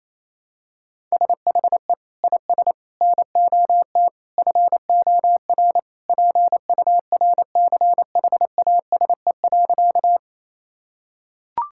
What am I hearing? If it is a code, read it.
Morse code, heard as a keyed tone, about 28 wpm, 700 Hz. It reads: SHE IS NOT FOR PURCHASE.
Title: Sentences from Top 1000 Words - Review Part 5 - 28wpm